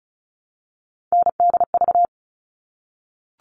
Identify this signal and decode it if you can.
Morse code, heard as a keyed tone, about 34 wpm, 700 Hz. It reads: NDV
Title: All Cut Numbers - 3-Digit 35wpm